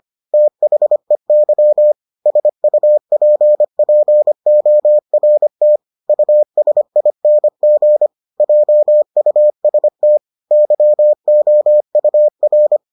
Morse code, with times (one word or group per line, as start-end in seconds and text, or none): 0.3-1.9 THEY
2.3-5.8 SUPPORT
6.1-8.1 USING
8.4-10.2 JUST
10.5-12.8 YOUR